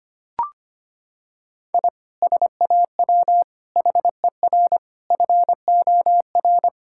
Morse code, heard as a keyed tone, about 25 wpm, 700 Hz. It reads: I SAW HER FOR
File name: practice / 25wpm / Sentences with 4-Word Phrases - 25wpm